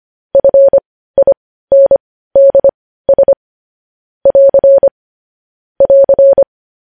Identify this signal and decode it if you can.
Morse code, heard as a keyed tone, about 25 wpm, spaced as for 15 wpm, 550 Hz. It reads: FINDS <AR> <AR>
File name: podcast / News headlines-1x-25x15